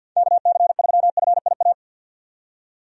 Morse code, heard as a keyed tone, about 46 wpm, 700 Hz. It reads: KC3FIA
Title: USA Call Signs - 6 Characters - 50wpm